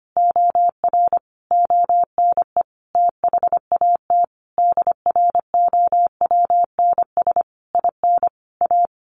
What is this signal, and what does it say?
Morse code, heard as a keyed tone, about 25 wpm, 700 Hz. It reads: OR ONE THAT DROWNS IN A
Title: A Sound Of Thunder 25wpm